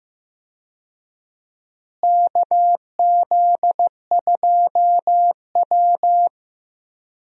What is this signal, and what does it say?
Morse code, heard as a keyed tone, about 15 wpm, 700 Hz. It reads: KZ2W
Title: USA Call Signs - 4 Characters - Repeated 3x - 15wpm